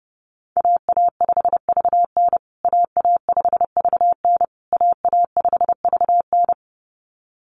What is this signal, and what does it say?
Morse code, heard as a keyed tone, about 29 wpm, 700 Hz. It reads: AA5VN AA5VN AA5VN
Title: USA Call Signs Repeated 3x 30wpm